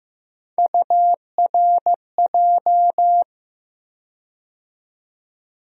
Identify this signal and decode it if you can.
Morse code, heard as a keyed tone, about 15 wpm, 700 Hz. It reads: URJ
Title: USA Call Sign Postfixes - 3 Letters 15wpm